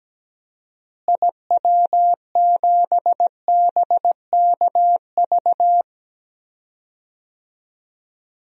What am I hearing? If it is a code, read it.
Morse code, heard as a keyed tone, about 17 wpm, 700 Hz. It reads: IW7BKV
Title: DXCC-100 Call Signs - 6 Characters - Repeated 3x - 17wpm